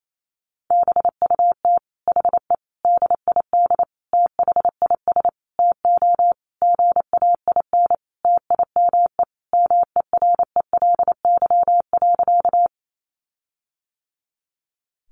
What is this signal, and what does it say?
Morse code, heard as a keyed tone, about 28 wpm, 700 Hz. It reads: BUT HE DID THIS TO GAIN TIME MERELY.
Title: Sentences from Top 1400 Words 28wpm